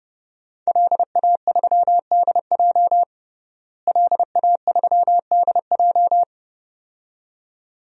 Morse code, heard as a keyed tone, about 30 wpm, 700 Hz. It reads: LA3DJ LA3DJ
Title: DXCC-100 Call Signs - 5 Characters - Repeated 2x - 30wpm